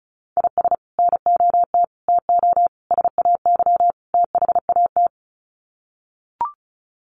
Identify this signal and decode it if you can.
Morse code, heard as a keyed tone, about 34 wpm, 700 Hz. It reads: IS NOT TO SAY THAT
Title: Sets of 5 Words 35wpm